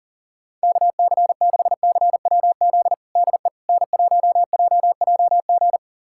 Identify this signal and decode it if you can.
Morse code, heard as a keyed tone, about 37 wpm, 700 Hz. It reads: KC6CWZ DE N1JJG
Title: USA Callsign DE Callsign 40wpm